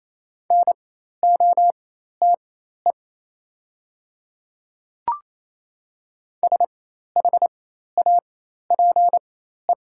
Morse code, heard as a keyed tone, about 28 wpm, spaced as for 13 wpm, 700 Hz. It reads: NOTE SHAPE
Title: 4x Character Spacing - Top 500 Words - 28wpm